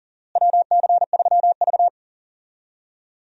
Morse code, heard as a keyed tone, about 40 wpm, 700 Hz. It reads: WC3V
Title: USA Call Signs - 4 Characters - Repeated 3x - 40wpm